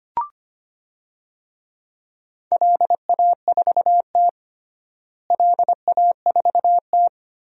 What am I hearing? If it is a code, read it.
Morse code, heard as a keyed tone, about 25 wpm, 700 Hz. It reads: LA4T LA4T
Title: DXCC-100 Call Signs - 4 Characters - Repeated 3x - 25wpm